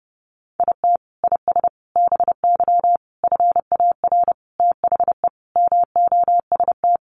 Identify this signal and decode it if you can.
Morse code, heard as a keyed tone, about 28 wpm, 700 Hz. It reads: IT IS BY FAR THE MOST
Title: Sentences from Top 1200 Words 30wpm